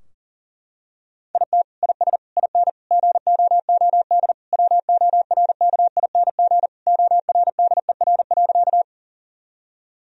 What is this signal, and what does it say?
Morse code, heard as a keyed tone, about 40 wpm, 700 Hz. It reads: IT IS IN GOOD WORKING ORDER.
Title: Sentences from Top 1000 Words - Review Part 4 - 40wpm